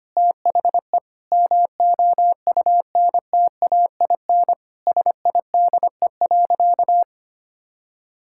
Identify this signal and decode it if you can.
Morse code, heard as a keyed tone, about 25 wpm, 700 Hz. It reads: THE MOUNTAIN SIDE.
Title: Sentences from Top 1000 Words - Review Part 6 - 25wpm